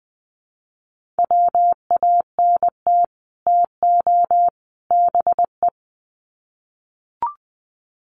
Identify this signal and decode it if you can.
Morse code, heard as a keyed tone, about 20 wpm, 700 Hz. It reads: WANT TO BE
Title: Sets of 3 Words 20wpm